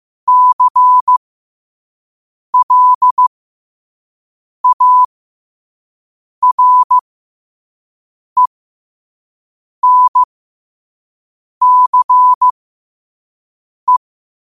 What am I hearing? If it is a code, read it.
Morse code, heard as a keyed tone, about 15 wpm, spaced as for 5 wpm, 1000 Hz. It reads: CLARENCE
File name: qotdfiles / 5wpm